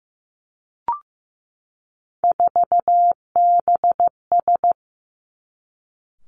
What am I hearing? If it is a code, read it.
Morse code, heard as a keyed tone, about 15 wpm, 700 Hz. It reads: 4BS
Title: Three Challenging Letter-Number Combinations 15wpm